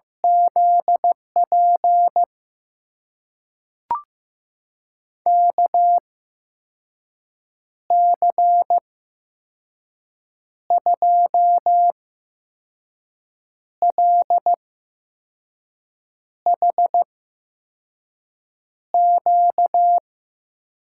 Morse code, heard as a keyed tone, about 15 wpm, spaced as for 5 wpm, 700 Hz. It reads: ZP KC2LHQ
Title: USA Call Signs - 6 Characters - 8x Character Spacing - 15wpm